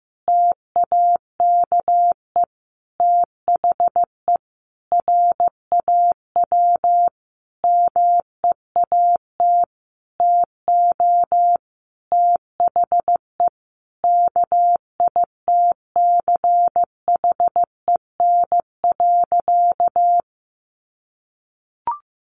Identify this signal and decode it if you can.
Morse code, heard as a keyed tone, about 15 wpm, 700 Hz. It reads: TAKE THE RAW MEAT TO THE KITCHEN.
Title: Sentences from Top 1500 Words 15wpm